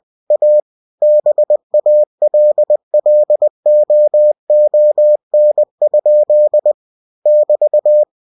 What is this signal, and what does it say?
Morse code, heard as a keyed tone, about 20 wpm, 600 Hz. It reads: A BALLOON? <BT>
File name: cw-books / winnie-the-pooh / winnie-20wpm-page-0004